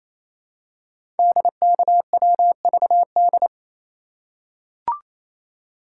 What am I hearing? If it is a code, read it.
Morse code, heard as a keyed tone, about 27 wpm, 700 Hz. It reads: DKWVD
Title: Code Groups - 5 Letters - 28wpm